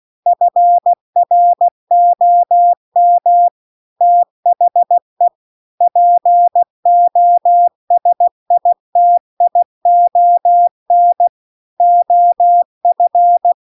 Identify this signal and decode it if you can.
Morse code, heard as a keyed tone, about 16 wpm, 700 Hz. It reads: FROM THE POSITION OF